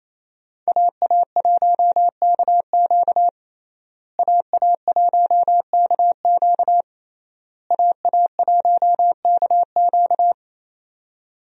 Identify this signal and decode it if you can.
Morse code, heard as a keyed tone, about 28 wpm, 700 Hz. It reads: AA1KQ AA1KQ AA1KQ
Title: USA Call Signs - 5 Characters - Repeated 3x - 28wpm